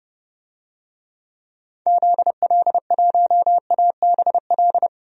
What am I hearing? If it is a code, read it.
Morse code, heard as a keyed tone, about 30 wpm, 700 Hz. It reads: ZL1ABL
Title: DXCC-100 Call Signs - 6 Characters - 30wpm